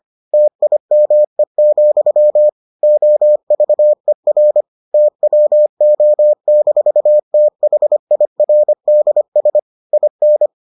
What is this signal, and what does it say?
Morse code, heard as a keyed tone, about 25 wpm, 600 Hz. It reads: TIME, OVER TWO-THIRDS IN